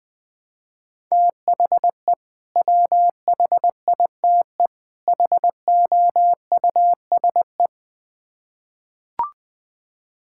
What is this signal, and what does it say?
Morse code, heard as a keyed tone, about 21 wpm, 700 Hz. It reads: THE WHITE HOUSE
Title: Build A Phrase - 3-Word Phrases - 20wpm